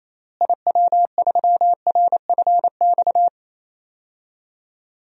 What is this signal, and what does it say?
Morse code, heard as a keyed tone, about 27 wpm, 700 Hz. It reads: IW3RFX
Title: DXCC-100 Call Signs - 6 Characters - Repeated 2x - 28wpm